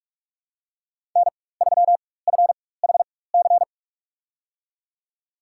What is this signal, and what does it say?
Morse code, heard as a keyed tone, about 42 wpm, spaced as for 21 wpm, 700 Hz. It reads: N3FHC
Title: USA Call Signs - 5 Characters - 4x Character Spacing - 45wpm